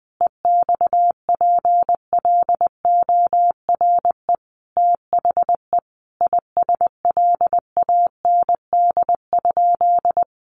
Morse code, heard as a keyed tone, about 20 wpm, 700 Hz. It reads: EXPLORE THE ISLAND?
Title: Sentences from Top 1100 Words 20wpm